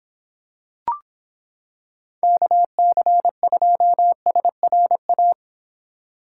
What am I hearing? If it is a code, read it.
Morse code, heard as a keyed tone, about 26 wpm, 700 Hz. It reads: KC2SRA